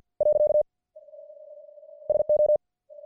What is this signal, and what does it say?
Morse code, heard as a keyed tone, about 50 wpm, 600 Hz. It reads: . SO